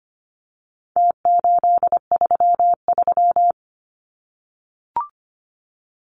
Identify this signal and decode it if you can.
Morse code, heard as a keyed tone, about 25 wpm, 700 Hz. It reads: T833
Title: Numbers - Contest Style - 4-Digit 25wpm